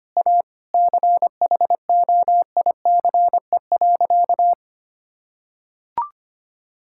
Morse code, heard as a keyed tone, about 25 wpm, 700 Hz. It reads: A CHOICE.
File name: practice / 25wpm / Sentences with 3-Word Phrases - 25wpm